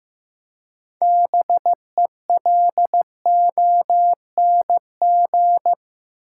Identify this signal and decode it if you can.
Morse code, heard as a keyed tone, about 15 wpm, 700 Hz. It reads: BELONG
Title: Six Letter Words - 15wpm